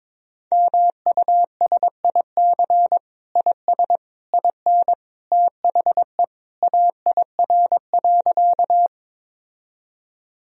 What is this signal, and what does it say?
Morse code, heard as a keyed tone, about 21 wpm, 700 Hz. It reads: MUSIC IS IN THE AIR.
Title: Sentences from Top 1000 Words - Review Part 1 - 22wpm